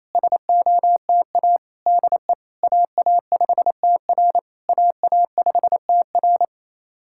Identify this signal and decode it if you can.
Morse code, heard as a keyed tone, about 28 wpm, 700 Hz. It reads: SOTA DE AA5TR AA5TR
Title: SOTA QSO Practice 28wpm